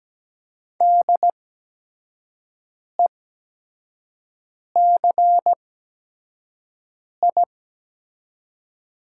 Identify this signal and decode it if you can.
Morse code, heard as a keyed tone, about 17 wpm, spaced as for 5 wpm, 700 Hz. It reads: DECI
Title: 8x Character Spacing - Top 500 Words - 17wpm